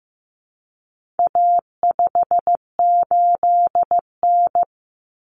Morse code, heard as a keyed tone, about 15 wpm, 700 Hz. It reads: A58N
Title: Common Cut Numbers - 4-Digit 15wpm